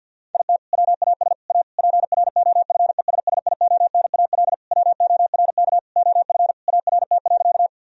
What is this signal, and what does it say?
Morse code, heard as a keyed tone, about 50 wpm, 700 Hz. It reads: IT WAS A PROFESSIONAL WORK OF ART.
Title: Sentences from Top 1000 Words - Review Part 4 - 50wpm